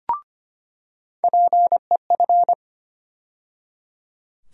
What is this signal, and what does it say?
Morse code, heard as a keyed tone, about 25 wpm, 700 Hz. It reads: PEF
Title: USA Call Sign Postfixes - 3 Letters 25wpm